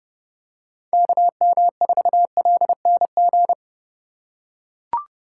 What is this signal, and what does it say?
Morse code, heard as a keyed tone, about 30 wpm, 700 Hz. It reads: KM4LNG
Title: USA Call Signs - 6 Characters - 6x Character Spacing - 30wpm